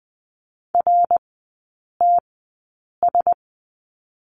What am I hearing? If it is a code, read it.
Morse code, heard as a keyed tone, about 20 wpm, spaced as for 8 wpm, 700 Hz. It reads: RTS
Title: 081 Lesson 14e - Number 2 - Three Characters 20wpm